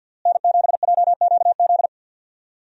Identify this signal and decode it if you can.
Morse code, heard as a keyed tone, about 52 wpm, 700 Hz. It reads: N7JQZ
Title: USA Call Signs - 5 Characters - Repeated 3x - 50wpm